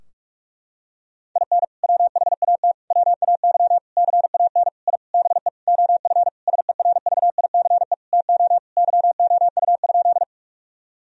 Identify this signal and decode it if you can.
Morse code, heard as a keyed tone, about 41 wpm, 700 Hz. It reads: IN WHAT WAY CAN I BE OF SERVICE TO YOU?
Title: Sentences from Top 1000 Words - Review Part 4 - 45wpm